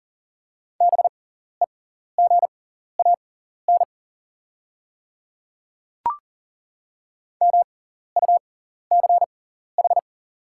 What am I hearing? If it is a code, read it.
Morse code, heard as a keyed tone, about 40 wpm, spaced as for 14 wpm, 700 Hz. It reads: BEGAN MUCH
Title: 6x Character Spacing - Top 500 Words - 40wpm